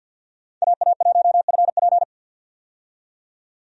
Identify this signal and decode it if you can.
Morse code, heard as a keyed tone, about 50 wpm, 700 Hz. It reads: AA1FP
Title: USA Call Signs - 5 Characters - 50wpm